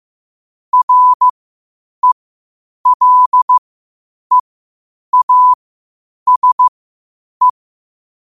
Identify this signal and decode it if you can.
Morse code, heard as a keyed tone, about 15 wpm, spaced as for 8 wpm, 1000 Hz. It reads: RELEASE
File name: qotdfiles / 8wpm